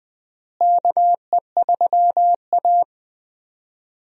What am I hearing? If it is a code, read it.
Morse code, heard as a keyed tone, about 20 wpm, 700 Hz. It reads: KE3A
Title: USA Call Signs - 4 Characters - Repeated 2x - 20wpm